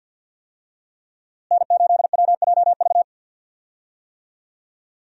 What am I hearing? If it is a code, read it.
Morse code, heard as a keyed tone, about 52 wpm, 700 Hz. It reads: N8WJV
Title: USA Call Signs - 5 Characters - Repeated 3x - 50wpm